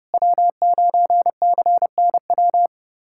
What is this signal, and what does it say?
Morse code, heard as a keyed tone, about 30 wpm, 700 Hz. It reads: W9CNW
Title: USA Callsign DE Callsign - 5 Characters - 30wpm